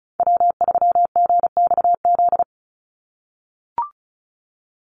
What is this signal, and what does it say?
Morse code, heard as a keyed tone, about 34 wpm, 700 Hz. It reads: W3GXZ
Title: USA Call Signs 35wpm